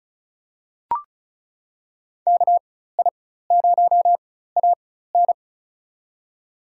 Morse code, heard as a keyed tone, about 36 wpm, spaced as for 16 wpm, 700 Hz. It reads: KI0AN
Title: USA Call Signs - 5 Characters - 4x Character Spacing - 35wpm